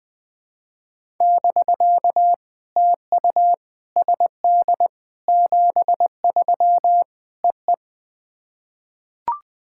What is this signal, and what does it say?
Morse code, heard as a keyed tone, about 20 wpm, 700 Hz. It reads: <BK> TU SD 73 EE
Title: POTA QSO Practice - 20wpm